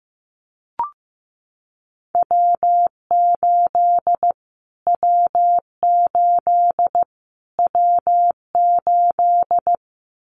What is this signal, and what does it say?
Morse code, heard as a keyed tone, about 15 wpm, 700 Hz. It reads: W8 W8 W8
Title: Callsign Prefixes for Top 25 DXCC - Repeated 3x - 15wpm